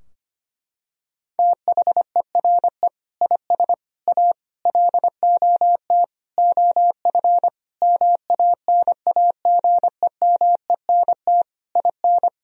Morse code, heard as a keyed tone, about 25 wpm, 700 Hz. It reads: THERE IS A LOT OF MANAGEMENT IN
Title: Sentences from Top 1000 Words - Review Part 4 - 25wpm